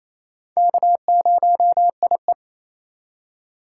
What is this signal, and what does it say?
Morse code, heard as a keyed tone, about 28 wpm, 700 Hz. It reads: K0IE
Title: USA Call Signs - 4 Characters - 3x Character Spacing - 28wpm